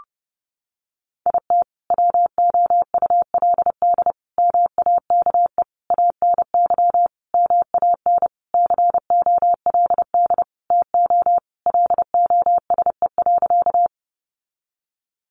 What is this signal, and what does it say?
Morse code, heard as a keyed tone, about 30 wpm, 700 Hz. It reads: IT WOULD MAKE ANY MAN COLD TO LOSE.
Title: Sentences from Top 1100 Words 30wpm